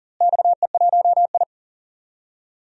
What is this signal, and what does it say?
Morse code, heard as a keyed tone, about 39 wpm, 700 Hz. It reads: XE1I
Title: DXCC-100 Call Signs - 4 Characters - Repeated 3x - 40wpm